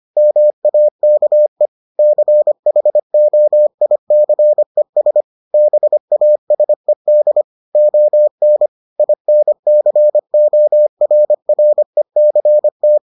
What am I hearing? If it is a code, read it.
Morse code, heard as a keyed tone, about 25 wpm, 600 Hz. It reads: MAKE CHOICES BASED ON INCORRECT